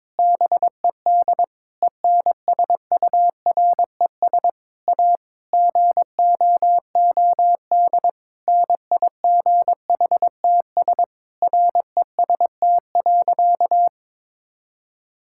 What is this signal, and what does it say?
Morse code, heard as a keyed tone, about 22 wpm, 700 Hz. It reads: BED ENSURES A GOOD NIGHTS REST.
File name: Difficult Sentences with Hints - 22wpm